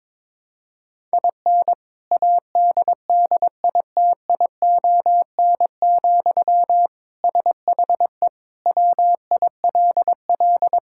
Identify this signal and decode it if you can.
Morse code, heard as a keyed tone, about 22 wpm, 700 Hz. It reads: IN ADDITION, SHE WILL
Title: Sentences from Top 1000 Words - Review Part 6 - 22wpm